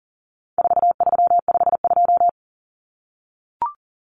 Morse code, heard as a keyed tone, about 40 wpm, 700 Hz. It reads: 4352
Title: Common Cut Numbers - 4-Digit 40wpm